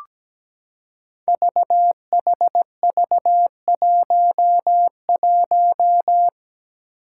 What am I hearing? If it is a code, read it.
Morse code, heard as a keyed tone, about 17 wpm, 700 Hz. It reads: VHV11
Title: Five Challenging Letter-Number Combinations 17wpm